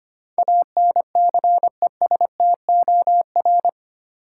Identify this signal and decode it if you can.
Morse code, heard as a keyed tone, about 25 wpm, 700 Hz. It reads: ANCESTOR
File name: Eight Letter Words - 25wpm